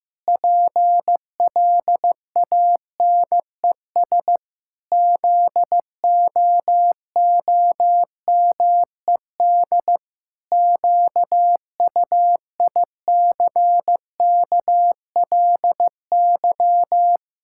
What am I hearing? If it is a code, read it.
Morse code, heard as a keyed tone, about 15 wpm, 700 Hz. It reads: PLANES ZOOMED QUICKLY
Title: Pangram Challenge v2 - 15wpm